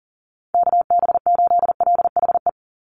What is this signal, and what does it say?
Morse code, heard as a keyed tone, about 40 wpm, 700 Hz. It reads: KB8LHE
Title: USA Call Signs 40wpm